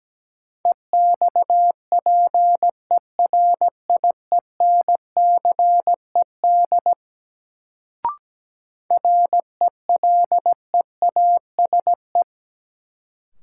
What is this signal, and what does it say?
Morse code, heard as a keyed tone, about 17 wpm, 700 Hz. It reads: EXPERIENCED RELEASE
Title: Sentences from Top 1100 Words - New Word Drill - 17wpm